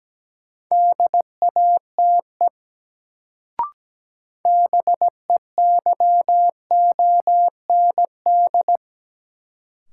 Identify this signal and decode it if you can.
Morse code, heard as a keyed tone, about 17 wpm, 700 Hz. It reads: DATE BEYOND
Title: Sentences from Top 1100 Words - New Word Drill Alt - 17wpm